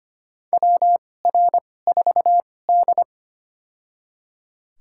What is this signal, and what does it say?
Morse code, heard as a keyed tone, about 25 wpm, spaced as for 18 wpm, 700 Hz. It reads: WR4D
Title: USA Call Signs - 4 Characters - 2x Character Spacing - 25wpm